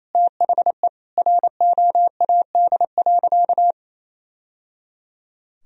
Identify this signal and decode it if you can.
Morse code, heard as a keyed tone, about 28 wpm, 700 Hz. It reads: THE ROAD.
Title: Sentences from Top 1000 Words - Review Part 2 - 28wpm